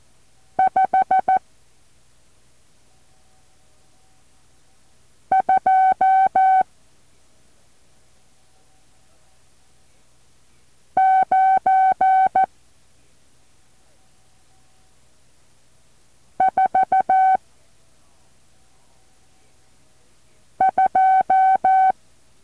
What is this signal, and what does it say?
Morse code, heard as a keyed tone, about 14 wpm, 760 Hz. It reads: 5 2 9 4 2
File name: intl morse code2